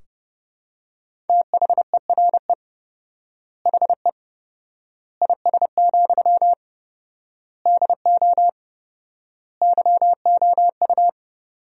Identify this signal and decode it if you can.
Morse code, heard as a keyed tone, about 30 wpm, 700 Hz. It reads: THERE HE IS, DO YOU
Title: 4x Word-Spacing - Sentences from Top 100 Words - 30wpm